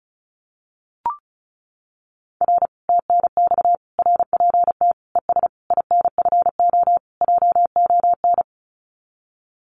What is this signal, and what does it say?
Morse code, heard as a keyed tone, about 36 wpm, 700 Hz. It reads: R TNX RPT ES INFO JON
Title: Common QSO Phrases Encore 35wpm